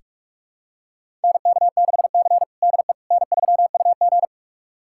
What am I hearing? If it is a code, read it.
Morse code, heard as a keyed tone, about 46 wpm, 700 Hz. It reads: NK6C DE N3UG
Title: USA Callsign DE Callsign - 4 Characters - 45wpm